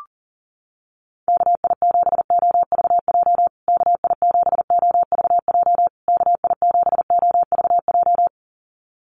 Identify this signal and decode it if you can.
Morse code, heard as a keyed tone, about 37 wpm, 700 Hz. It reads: KI7OVJ KI7OVJ KI7OVJ
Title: USA Call Signs Repeated 3x 40wpm